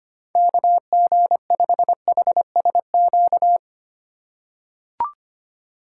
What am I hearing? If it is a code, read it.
Morse code, heard as a keyed tone, about 25 wpm, 700 Hz. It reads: KG5HSQ
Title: USA Call Signs - 6 Characters - 3x Character Spacing - 25wpm